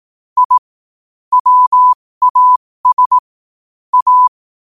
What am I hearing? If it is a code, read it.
Morse code, heard as a keyed tone, about 18 wpm, spaced as for 15 wpm, 1000 Hz. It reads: I WAS A
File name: qotdfiles / 15wpm